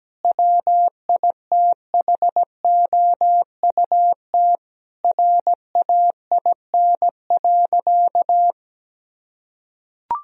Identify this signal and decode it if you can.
Morse code, heard as a keyed tone, about 17 wpm, 700 Hz. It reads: WITHOUT RAIN.
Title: Sentences from Top 1000 Words - Review Part 3 - 17wpm